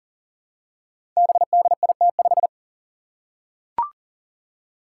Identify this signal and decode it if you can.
Morse code, heard as a keyed tone, about 38 wpm, 700 Hz. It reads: BDIT5
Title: Five Challenging Letter-Number Combinations 40wpm